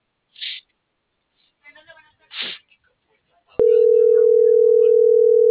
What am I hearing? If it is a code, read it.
Ringback tone (North America), 440 + 480 Hz.
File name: CENTRAL SAGOT / out isai.hernandez